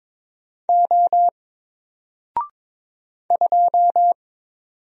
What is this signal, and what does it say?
Morse code, heard as a keyed tone, about 22 wpm, 700 Hz. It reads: O 2
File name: Single-Letter-Number-22wpm